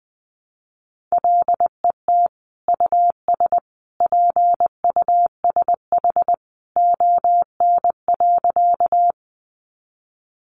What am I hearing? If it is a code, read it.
Morse code, heard as a keyed tone, about 20 wpm, 700 Hz. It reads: LET US PUSH ON.
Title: Sentences from Top 1300 Words 20wpm